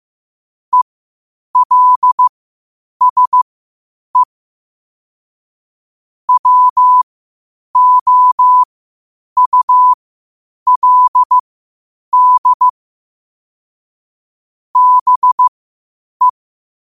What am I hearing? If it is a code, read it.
Morse code, heard as a keyed tone, about 15 wpm, spaced as for 8 wpm, 1000 Hz. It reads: ELSE WOULD BE
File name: qotdfiles / 8wpm